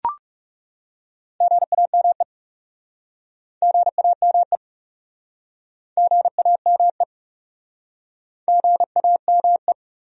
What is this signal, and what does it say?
Morse code, heard as a keyed tone, about 39 wpm, 700 Hz. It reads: GAME GAME GAME GAME